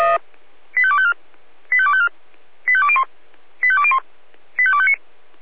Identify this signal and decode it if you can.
Five-tone selective call (ZVEI), heard as quick runs of five tones, tones of about 70 ms. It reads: 85254 85254 85201 85201 85259